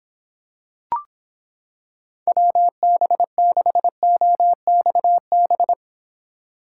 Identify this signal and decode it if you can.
Morse code, heard as a keyed tone, about 26 wpm, 700 Hz. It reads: WB6OXB